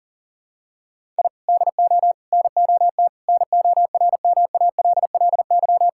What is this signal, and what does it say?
Morse code, heard as a keyed tone, about 40 wpm, 700 Hz. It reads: I DO NOT NORMALLY
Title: Sentences from Top 1000 Words - Review Part 6 - 40wpm